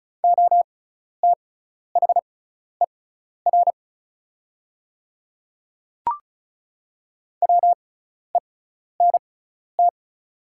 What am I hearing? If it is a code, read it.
Morse code, heard as a keyed tone, about 36 wpm, spaced as for 12 wpm, 690 Hz. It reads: OTHER WENT
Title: 6x Character Spacing - Top 300 Words - 35wpm